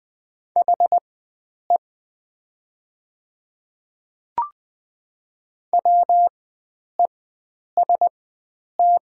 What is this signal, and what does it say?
Morse code, heard as a keyed tone, about 20 wpm, spaced as for 9 wpm, 700 Hz. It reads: HE WEST
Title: 4x Character Spacing - Top 500 Words - 20wpm